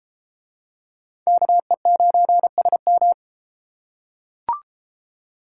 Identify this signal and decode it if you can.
Morse code, heard as a keyed tone, about 29 wpm, 700 Hz. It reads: KE9SM